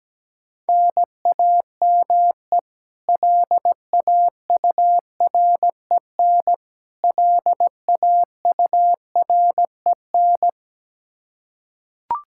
Morse code, heard as a keyed tone, about 17 wpm, 700 Hz. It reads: NAME LAUREN LAUREN
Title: Practice QSO - Exchange 1 - Beginning - 17wpm